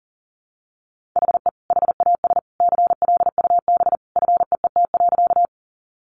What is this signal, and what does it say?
Morse code, heard as a keyed tone, about 40 wpm, 700 Hz. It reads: HE HAS CLUB FEET.
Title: Sentences from Top 1200 Words 40wpm